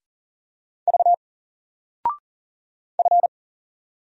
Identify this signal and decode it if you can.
Morse code, heard as a keyed tone, about 40 wpm, 700 Hz. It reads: V F
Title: Single Letter-Number 40wpm